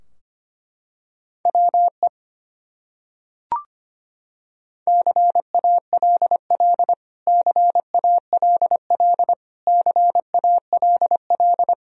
Morse code, heard as a keyed tone, about 25 wpm, 700 Hz. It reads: WE CALL CALL CALL
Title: Top 100 Words Repeated 25wpm